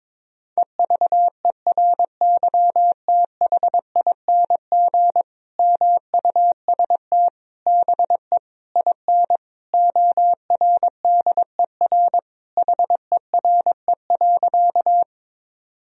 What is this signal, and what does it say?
Morse code, heard as a keyed tone, about 22 wpm, 700 Hz. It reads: EVERYTHING MUST BE IN ORDER HERE.
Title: Sentences from Top 1000 Words - Review Part 2 - 22wpm